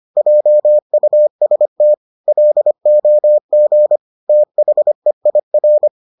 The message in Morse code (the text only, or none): JUST LOG THEIR